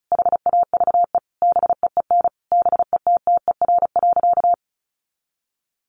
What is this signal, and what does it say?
Morse code, heard as a keyed tone, about 34 wpm, 700 Hz. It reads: HAVE BEEN BETTER.
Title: Sentences from Top 500 Words 35wpm